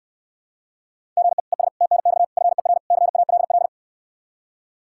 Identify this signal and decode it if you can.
Morse code, heard as a keyed tone, about 69 wpm, 700 Hz. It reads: GE ES TNX FER CALL